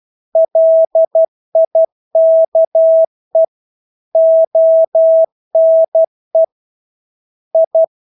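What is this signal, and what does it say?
Morse code, heard as a keyed tone, about 12 wpm, 650 Hz. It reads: LIKE ONE I
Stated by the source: Morse practice of English words at 12 wpm.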